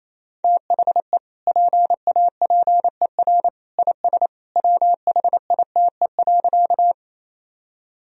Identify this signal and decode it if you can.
Morse code, heard as a keyed tone, about 28 wpm, 700 Hz. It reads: THE PAPER IS WHITE.
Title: Sentences from Top 1000 Words - Review Part 1 - 28wpm